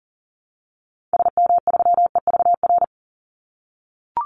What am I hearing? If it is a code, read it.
Morse code, heard as a keyed tone, about 40 wpm, 700 Hz. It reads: SM3EVR
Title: Call Signs - 40wpm